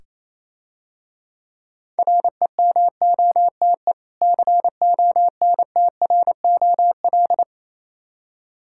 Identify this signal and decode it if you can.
Morse code, heard as a keyed tone, about 27 wpm, 690 Hz. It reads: REMOTE CONTROL
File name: Easy Words - Speed-Racing with Hints - 22wpm